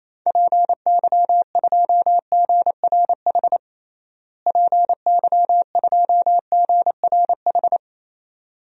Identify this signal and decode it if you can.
Morse code, heard as a keyed tone, about 28 wpm, 700 Hz. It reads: PY2GRH PY2GRH
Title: DXCC-100 Call Signs - 6 Characters - Repeated 2x - 28wpm